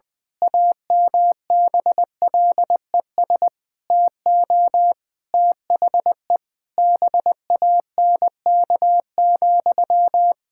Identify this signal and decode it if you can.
Morse code, heard as a keyed tone, about 20 wpm, 700 Hz. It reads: AMBLES TO THE BANK,
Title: Pangram Challenge v2 - 20wpm